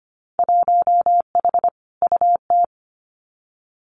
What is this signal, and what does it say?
Morse code, heard as a keyed tone, about 25 wpm, 700 Hz. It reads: 1H UT
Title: Winter Field Day Exchanges 25wpm